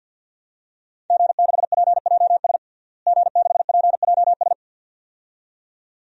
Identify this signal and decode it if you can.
Morse code, heard as a keyed tone, about 47 wpm, 700 Hz. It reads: G6PJS G6PJS
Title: DXCC-100 Call Signs - 5 Characters - Repeated 2x - 50wpm